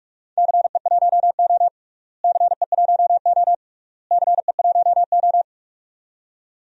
Morse code, heard as a keyed tone, about 47 wpm, 700 Hz. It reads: CE1O CE1O CE1O
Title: DXCC-100 Call Signs - 4 Characters - Repeated 3x - 45wpm